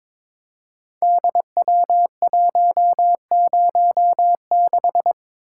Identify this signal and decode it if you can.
Morse code, heard as a keyed tone, about 21 wpm, 700 Hz. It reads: DW106
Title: Code Groups - 5 Letter:Numbers - 22wpm